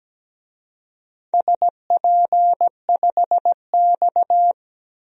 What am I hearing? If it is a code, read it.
Morse code, heard as a keyed tone, about 17 wpm, 700 Hz. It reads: SP5X
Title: DXCC-100 Call Signs - 4 Characters - Repeated 2x - 17wpm